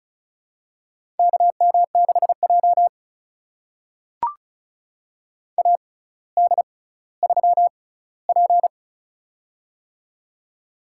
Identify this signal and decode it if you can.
Morse code, heard as a keyed tone, about 36 wpm, spaced as for 18 wpm, 700 Hz. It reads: KM6J AD3P
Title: USA Call Signs - 4 Characters - 6x Character Spacing - 35wpm